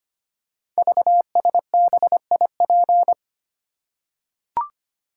Morse code, heard as a keyed tone, about 25 wpm, 700 Hz. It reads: VSBIP
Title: Code Groups - 5 Letter:Numbers - 25wpm